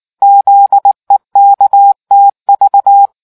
Morse code, heard as a keyed tone, about 19 wpm, 800 Hz. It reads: ZEKTV